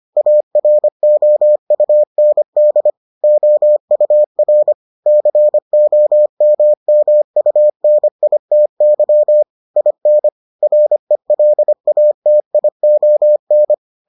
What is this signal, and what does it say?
Morse code, heard as a keyed tone, about 25 wpm, 600 Hz. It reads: AROUND OUR COMMUNITY IN RELATION